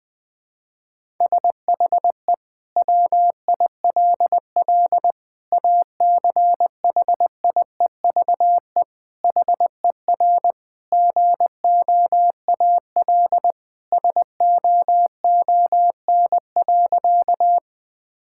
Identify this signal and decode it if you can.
Morse code, heard as a keyed tone, about 20 wpm, 700 Hz. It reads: SHE WILL ACHIEVE HER GOAL SOON.
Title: Sentences from Top 1000 Words - Review Part 6 - 20wpm